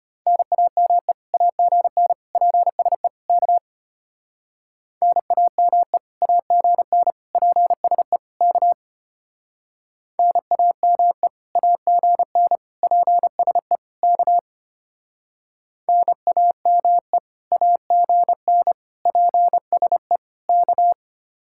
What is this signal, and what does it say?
Morse code, heard as a keyed tone, about 31 wpm, 700 Hz. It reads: NAME AGN PSE K NAME AGN PSE K NAME AGN PSE K NAME AGN PSE K
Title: Speed-Racing Common QSO Phrases - 25wpm